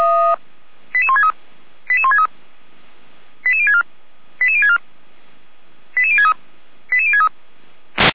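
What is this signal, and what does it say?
Five-tone selective call (ZVEI), heard as quick runs of five tones, tones of about 70 ms. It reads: 88163 88163 88064 88064 88063 88063
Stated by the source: ZVEI five-tone selective call received on SDR.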